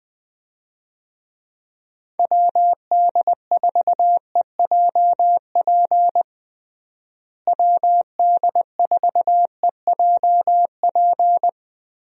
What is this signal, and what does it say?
Morse code, heard as a keyed tone, about 20 wpm, 700 Hz. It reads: WD4EJP WD4EJP
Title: USA Call Signs - 6 Characters - Repeated 3x - 20wpm